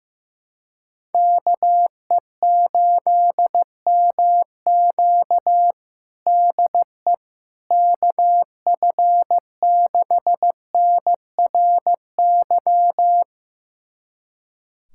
Morse code, heard as a keyed tone, about 15 wpm, 700 Hz. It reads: KE8MQ DE KF6NRY
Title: USA Callsign DE Callsign 15wpm